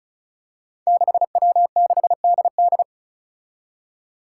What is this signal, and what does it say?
Morse code, heard as a keyed tone, about 36 wpm, 700 Hz. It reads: 6W6DD
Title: Five Challenging Letter-Number Combinations 35wpm